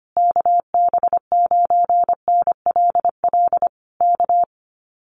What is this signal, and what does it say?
Morse code, heard as a keyed tone, about 25 wpm, 700 Hz. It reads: KB9NLL K
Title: Common QSO Phrases Encore 25wpm